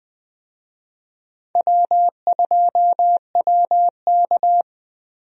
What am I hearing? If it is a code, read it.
Morse code, heard as a keyed tone, about 21 wpm, 700 Hz. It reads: W2WK
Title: USA Call Signs - 4 Characters - Repeated 3x - 20wpm